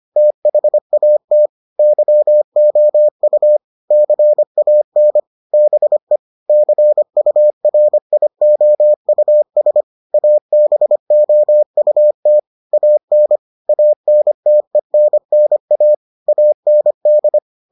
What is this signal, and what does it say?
Morse code, heard as a keyed tone, about 25 wpm, 600 Hz. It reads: THAT YOU CAN BE CURIOUS ABOUT AN ANTENNA AND